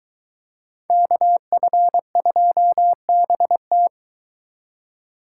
Morse code, heard as a keyed tone, about 23 wpm, 700 Hz. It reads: KF2BT